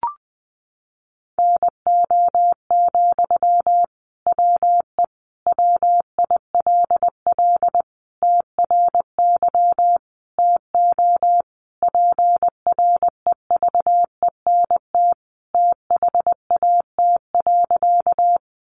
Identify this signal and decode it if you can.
Morse code, heard as a keyed tone, about 20 wpm, 700 Hz. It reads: NO, WE WILL TRY TO PREVENT THAT.
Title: Sentences from Top 1100 Words 20wpm